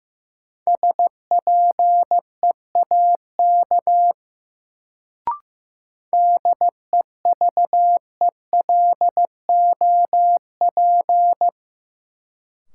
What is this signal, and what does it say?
Morse code, heard as a keyed tone, about 15 wpm, 700 Hz. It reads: SPEAK DEVELOP
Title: Sentences from Top 1000 Words - Review Part 2 - New Word Drill - 15wpm